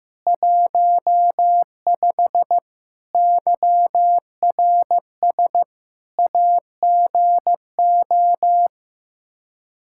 Morse code, heard as a keyed tone, about 15 wpm, 700 Hz. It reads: 15 YRS AGO
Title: Practice QSO - Exchange 3 - More Info - 15wpm